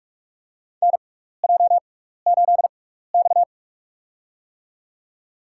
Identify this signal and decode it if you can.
Morse code, heard as a keyed tone, about 47 wpm, spaced as for 16 wpm, 700 Hz. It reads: NJ8X
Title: USA Call Signs - 4 Characters - 6x Character Spacing - 45wpm